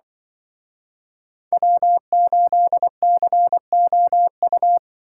Morse code, heard as a keyed tone, about 25 wpm, 700 Hz. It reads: W8COU
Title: Speed Racing - USA 1x3 Call Signs 20wpm